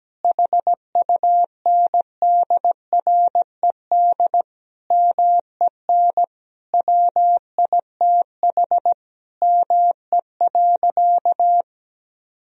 Morse code, heard as a keyed tone, about 17 wpm, 700 Hz. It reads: HUNDRED MEN WITH ME.
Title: Sentences from Top 1000 Words - Review Part 2 - 17wpm